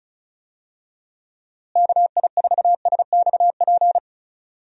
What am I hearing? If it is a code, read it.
Morse code, heard as a keyed tone, about 34 wpm, 700 Hz. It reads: KI4SXP
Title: USA Call Signs - 6 Characters - Repeated 3x - 35wpm